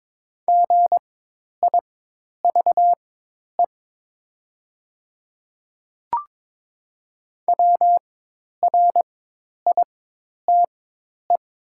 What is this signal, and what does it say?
Morse code, heard as a keyed tone, about 22 wpm, spaced as for 10 wpm, 700 Hz. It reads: GIVE WRITE
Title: 4x Character Spacing - Top 200 Words - 22wpm